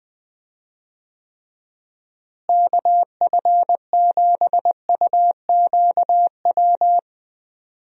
Morse code, heard as a keyed tone, about 21 wpm, 700 Hz. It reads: KF7UQW
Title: USA Call Signs - 6 Characters - Repeated 2x - 20wpm